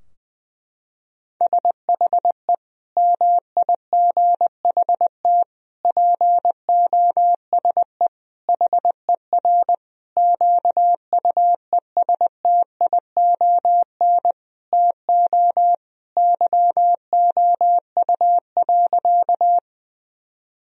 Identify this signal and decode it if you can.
Morse code, heard as a keyed tone, about 20 wpm, 700 Hz. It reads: SHE MIGHT POSE HER QUESTION TO YOU.
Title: Sentences from Top 1000 Words - Review Part 2 - 20wpm